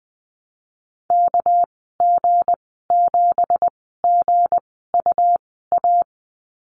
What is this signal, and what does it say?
Morse code, heard as a keyed tone, about 20 wpm, spaced as for 14 wpm, 700 Hz. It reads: KG7GUA
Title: 245 Lesson 39g - Letter z - Call Signs 20wpm